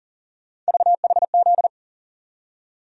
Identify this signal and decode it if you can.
Morse code, heard as a keyed tone, about 40 wpm, 700 Hz. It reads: VHZ
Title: USA Call Sign Postfixes - 3 Letters 40wpm